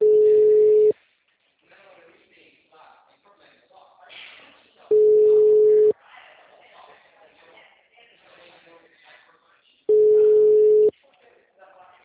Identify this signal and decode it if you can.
Ringback tone (Europe, 425 Hz), with speech.